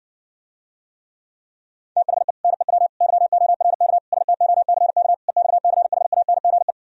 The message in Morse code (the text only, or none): THE NEW YORK STOCK EXCHANGE